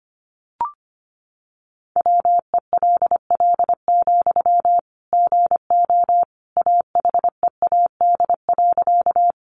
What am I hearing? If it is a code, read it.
Morse code, heard as a keyed tone, about 25 wpm, 700 Hz. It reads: WELL, GO AHEAD.
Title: Sentences from Top 1100 Words 25wpm